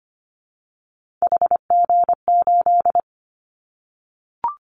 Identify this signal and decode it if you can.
Morse code, heard as a keyed tone, about 25 wpm, 700 Hz. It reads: HG8
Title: Callsign Prefixes for Top 25 DXCC - Repeated 3x - 25wpm